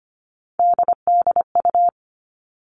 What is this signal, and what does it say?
Morse code, heard as a keyed tone, about 25 wpm, 700 Hz. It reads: DDU
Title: All Cut Numbers - 3-Digit 25wpm